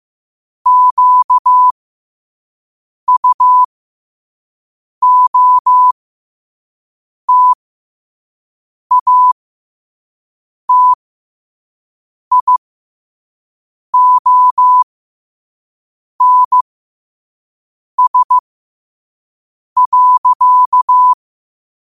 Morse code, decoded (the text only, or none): QUOTATIONS.